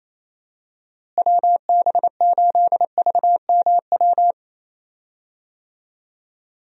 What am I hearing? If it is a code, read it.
Morse code, heard as a keyed tone, about 28 wpm, 700 Hz. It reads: WB8VMW
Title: USA Call Signs - 6 Characters - Repeated 2x - 28wpm